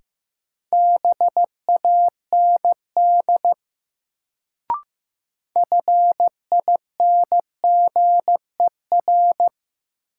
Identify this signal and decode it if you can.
Morse code, heard as a keyed tone, about 15 wpm, 700 Hz. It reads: BAND FINGER
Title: Sentences from Top 1600 Words - New Word Drill Alt - 15wpm